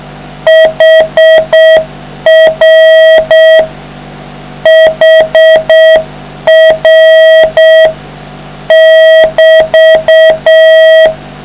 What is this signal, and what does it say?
Morse code, heard as a keyed tone, about 6 wpm, 660 Hz, with noise. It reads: HR HR <BT>